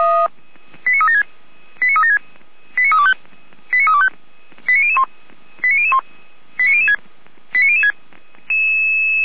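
Five-tone selective call (ZVEI), heard as quick runs of five tones, tones of about 70 ms. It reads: 89376 89376 89325 89325 89001 89001 89006 89006
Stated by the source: ZVEI five-tone call 89099 received on SDR, Feuerwehr Orlamünde.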